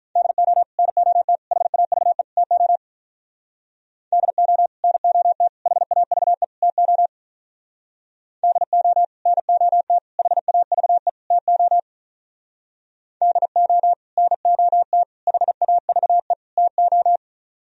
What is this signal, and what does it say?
Morse code, heard as a keyed tone, about 44 wpm, 700 Hz. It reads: DO NOT HAVE TO DO NOT HAVE TO DO NOT HAVE TO DO NOT HAVE TO